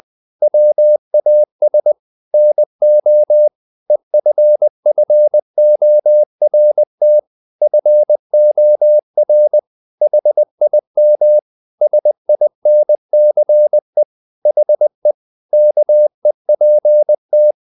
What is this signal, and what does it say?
Morse code, heard as a keyed tone, about 20 wpm, 600 Hz. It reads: WAS NO EFFORT FOR HIM SINCE HE KEPT
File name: the old man and the sea / the old man and the sea part 1 20wpm